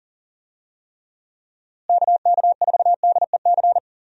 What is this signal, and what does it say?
Morse code, heard as a keyed tone, about 40 wpm, 700 Hz. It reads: KK4DEC
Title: USA Call Signs - 6 Characters - Repeated 2x - 40wpm